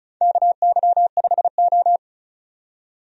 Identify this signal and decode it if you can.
Morse code, heard as a keyed tone, about 36 wpm, 700 Hz. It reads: KY5O
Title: USA Call Signs - 4 Characters - 35wpm